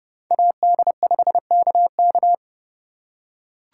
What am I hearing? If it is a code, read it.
Morse code, heard as a keyed tone, about 30 wpm, 700 Hz. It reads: AD5KK